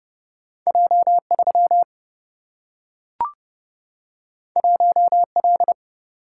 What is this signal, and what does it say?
Morse code, heard as a keyed tone, about 30 wpm, 700 Hz. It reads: J 3 1 L
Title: Single-Letter Single-Number 30wpm